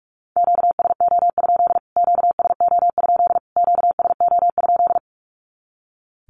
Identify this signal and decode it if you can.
Morse code, heard as a keyed tone, about 45 wpm, 700 Hz. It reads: QSO? QSO? QSO?
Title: Q-Codes Repeated 3x 45wpm